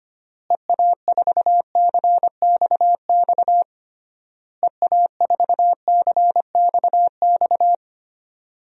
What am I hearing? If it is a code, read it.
Morse code, heard as a keyed tone, about 25 wpm, 700 Hz. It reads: EA4CXX EA4CXX
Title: DXCC-100 Call Signs - 6 Characters - Repeated 3x - 25wpm